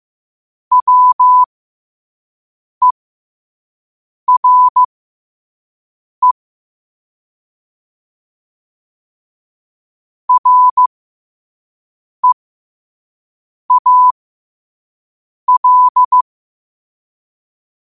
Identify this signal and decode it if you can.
Morse code, heard as a keyed tone, about 15 wpm, spaced as for 5 wpm, 1000 Hz. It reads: WERE REAL
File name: qotdfiles / 5wpm